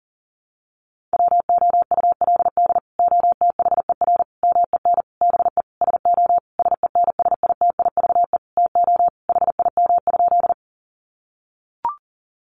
Morse code, heard as a keyed tone, about 40 wpm, 700 Hz. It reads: WOULD OTHER MEN BE SO SENSITIVE TO HIM?
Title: Sentences from Top 1400 Words 40wpm